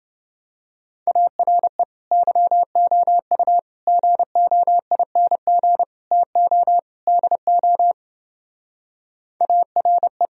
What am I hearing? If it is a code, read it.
Morse code, heard as a keyed tone, about 29 wpm, 700 Hz. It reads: ARE YOU GOING TO DO ARE